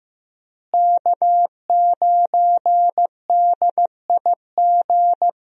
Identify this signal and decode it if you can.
Morse code, heard as a keyed tone, about 15 wpm, 700 Hz. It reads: K9DIG
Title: USA Call Signs - 5 Characters - Repeated 3x - 15wpm